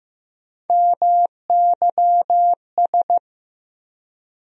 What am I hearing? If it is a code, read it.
Morse code, heard as a keyed tone, about 15 wpm, 700 Hz. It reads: MYS
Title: USA Call Sign Postfixes - 3 Letters 15wpm